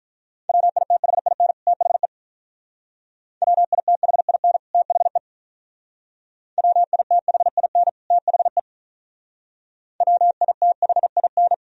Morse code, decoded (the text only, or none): WITHIN THE WITHIN THE WITHIN THE WITHIN